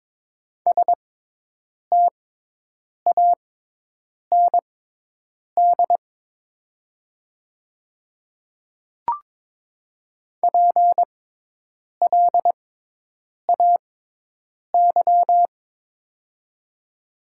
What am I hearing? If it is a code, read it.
Morse code, heard as a keyed tone, about 22 wpm, spaced as for 8 wpm, 700 Hz. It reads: STAND PLAY
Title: 6x Character Spacing - Top 300 Words - 22wpm